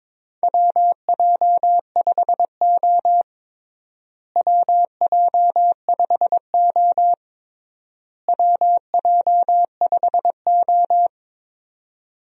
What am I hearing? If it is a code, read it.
Morse code, heard as a keyed tone, about 22 wpm, 700 Hz. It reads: WJ5O WJ5O WJ5O
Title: USA Call Signs - 4 Characters - Repeated 3x - 22wpm